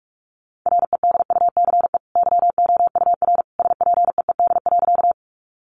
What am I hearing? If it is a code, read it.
Morse code, heard as a keyed tone, about 44 wpm, 700 Hz. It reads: REDUCE YOUR SPEED.
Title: Sentences from Top 1100 Words 45wpm